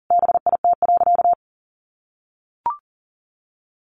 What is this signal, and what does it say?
Morse code, heard as a keyed tone, about 40 wpm, 700 Hz. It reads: BIT.
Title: Sentences from Top 900 Words 40wpm